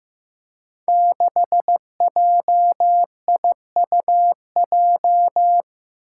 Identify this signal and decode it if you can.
Morse code, heard as a keyed tone, about 15 wpm, 700 Hz. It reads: 6JIUJ
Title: Five Challenging Letter-Number Combinations 15wpm